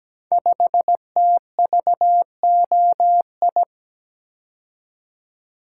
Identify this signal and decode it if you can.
Morse code, heard as a keyed tone, about 17 wpm, 700 Hz. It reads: 5TVOI
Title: Code Groups - 5 Letter:Numbers - 17wpm